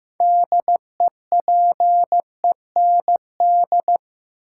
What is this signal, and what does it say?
Morse code, heard as a keyed tone, about 15 wpm, 700 Hz. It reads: DEPEND
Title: Sentences from Top 1300 Words - New Word Drill - 15wpm